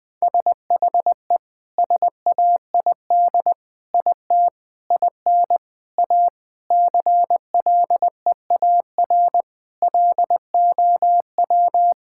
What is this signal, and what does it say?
Morse code, heard as a keyed tone, about 20 wpm, 700 Hz. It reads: SHE SAID IT IN A CLEAR LOW
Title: Sentences from Top 1600 Words 20wpm